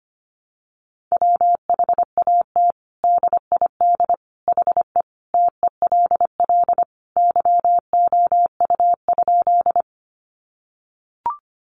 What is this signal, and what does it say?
Morse code, heard as a keyed tone, about 25 wpm, 700 Hz. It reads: WHAT DID HE TELL YOU?
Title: Sentences from Top 200 Words 25wpm